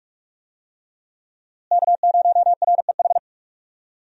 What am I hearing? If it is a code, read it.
Morse code, heard as a keyed tone, about 42 wpm, 700 Hz. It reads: K0REH
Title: USA Call Signs - 5 Characters - Repeated 2x - 45wpm